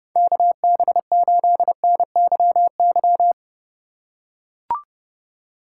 Morse code, heard as a keyed tone, about 31 wpm, 700 Hz. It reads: KB8NYY